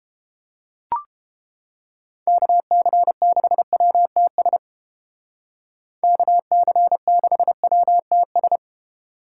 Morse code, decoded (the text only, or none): KC6WTS KC6WTS